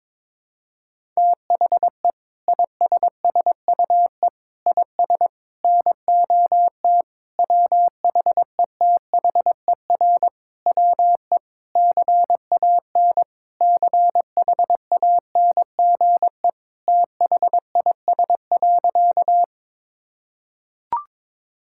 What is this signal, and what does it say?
Morse code, heard as a keyed tone, about 22 wpm, 700 Hz. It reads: THE ISSUE IS NOT WHETHER WE CAN CHANGE THIS.
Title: Sentences from Top 1000 Words - Review Part 6 - 22wpm